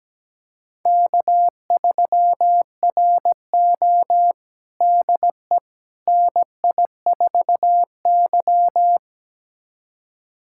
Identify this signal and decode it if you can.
Morse code, heard as a keyed tone, about 17 wpm, 700 Hz. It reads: K3RO DE NI4Y
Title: USA Callsign DE Callsign - 4 Characters - 17wpm